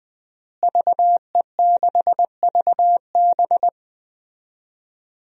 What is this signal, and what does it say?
Morse code, heard as a keyed tone, about 20 wpm, 700 Hz. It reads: VE6VB
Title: DXCC-100 Call Signs - 5 Characters - Repeated 3x - 20wpm